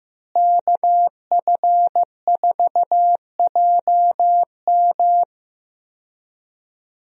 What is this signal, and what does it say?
Morse code, heard as a keyed tone, about 15 wpm, 700 Hz. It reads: KF4JM
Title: USA Call Signs - 5 Characters - Repeated 3x - 15wpm